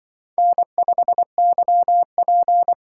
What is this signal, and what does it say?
Morse code, heard as a keyed tone, about 24 wpm, 700 Hz. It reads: N5YP
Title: Speed Racing - USA 1x2 Call Signs 20wpm